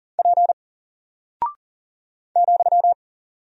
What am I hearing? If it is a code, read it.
Morse code, heard as a keyed tone, about 40 wpm, 700 Hz. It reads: P ,
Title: Single Character (Letter:Number:Punctuation:Prosign) Alt 40wpm